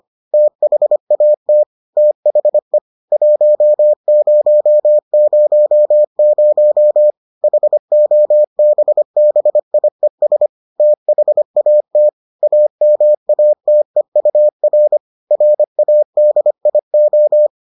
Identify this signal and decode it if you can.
Morse code, heard as a keyed tone, about 25 wpm, 600 Hz. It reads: THAT THE 1000 HOBBIES THAT AMATEUR RADIO